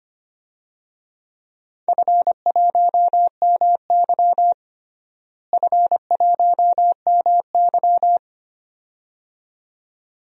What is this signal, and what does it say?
Morse code, heard as a keyed tone, about 25 wpm, 700 Hz. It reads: F1MY F1MY
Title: DXCC-100 Call Signs - 4 Characters - Repeated 2x - 25wpm